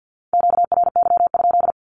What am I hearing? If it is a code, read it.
Morse code, heard as a keyed tone, about 50 wpm, 700 Hz. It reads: QRY?
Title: Q-Codes 50wpm